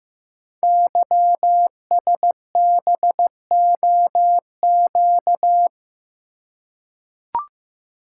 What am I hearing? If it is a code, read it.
Morse code, heard as a keyed tone, about 15 wpm, 700 Hz. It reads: YSBOQ
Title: Code Groups - 5 Letters - 15wpm